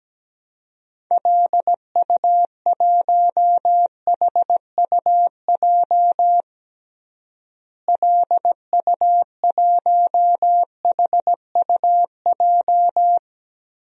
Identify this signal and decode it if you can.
Morse code, heard as a keyed tone, about 17 wpm, 700 Hz. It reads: LU1HUJ LU1HUJ
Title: DXCC-100 Call Signs - 6 Characters - Repeated 3x - 17wpm